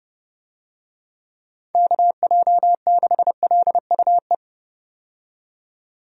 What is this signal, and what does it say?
Morse code, heard as a keyed tone, about 29 wpm, 700 Hz. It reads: KJ6LUE
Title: USA Call Signs - 6 Characters - 30wpm